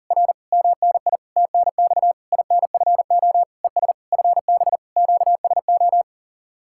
Morse code, heard as a keyed tone, about 40 wpm, 700 Hz. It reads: R MNI TNX INFO ES FB QSO